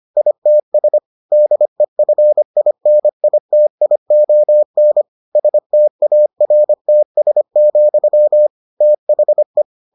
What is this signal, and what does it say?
Morse code, heard as a keyed tone, about 25 wpm, 600 Hz. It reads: ITS DEFINITION STARTS, THE